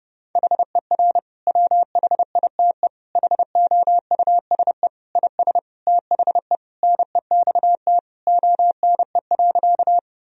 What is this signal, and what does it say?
Morse code, heard as a keyed tone, about 32 wpm, 700 Hz. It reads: HER WHITE HOUSE IS THE NEXT ONE.
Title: Sentences from Top 1000 Words - Review Part 1 - 30wpm